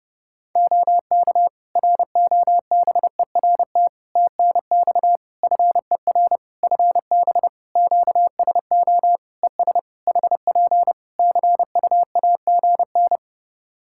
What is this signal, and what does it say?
Morse code, heard as a keyed tone, about 30 wpm, 700 Hz. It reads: OK ROBERT TNX FER FB QSO ES HP CUAGN
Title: Practice QSO - Exchange 4 - Ending - 30wpm